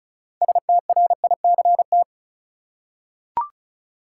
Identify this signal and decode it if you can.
Morse code, heard as a keyed tone, about 36 wpm, 700 Hz. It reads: STRICT
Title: Six Letter Words - 35wpm